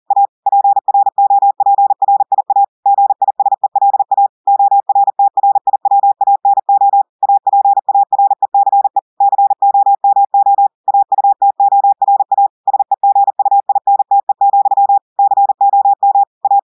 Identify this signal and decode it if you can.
Morse code, heard as a keyed tone, about 40 wpm, 800 Hz. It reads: A PROPRIA GISELA ORTRIWANO APARECE COMO AUTORA SEGUINTE, COM A
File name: audios / portugais / portugais texte en clair 1 vitesse 40 mots minutes